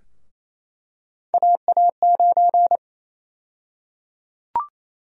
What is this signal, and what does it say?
Morse code, heard as a keyed tone, about 27 wpm, 700 Hz. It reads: AA9
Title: USA Call Sign Prefixes - 28wpm